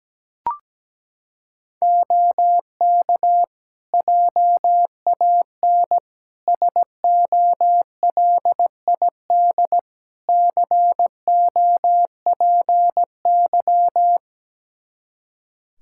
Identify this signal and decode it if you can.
Morse code, heard as a keyed tone, about 17 wpm, 700 Hz. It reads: OK JAN SOLID COPY
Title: K1USN SST QSO Practice 17wpm